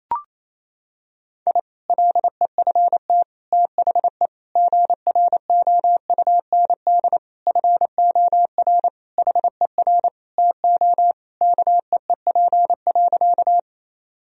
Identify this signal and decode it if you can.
Morse code, heard as a keyed tone, about 28 wpm, 700 Hz. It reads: I LEFT THE GROUND FOR HER TO KEEP.
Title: Sentences from Top 1000 Words - Review Part 2 - 28wpm